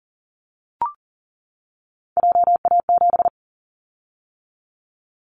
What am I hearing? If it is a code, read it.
Morse code, heard as a keyed tone, about 40 wpm, 700 Hz. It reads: JA7
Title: Callsign Prefixes for Top 50 DXCC - 40wpm